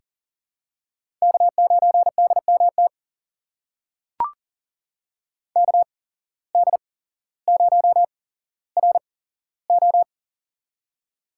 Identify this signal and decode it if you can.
Morse code, heard as a keyed tone, about 39 wpm, spaced as for 17 wpm, 700 Hz. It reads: K9DMT KD0RO
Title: USA Call Signs - 5 Characters - 8x Character Spacing - 40wpm